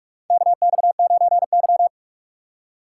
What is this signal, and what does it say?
Morse code, heard as a keyed tone, about 46 wpm, 700 Hz. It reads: KX9Y
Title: USA Call Signs - 4 Characters - 3x Character Spacing - 45wpm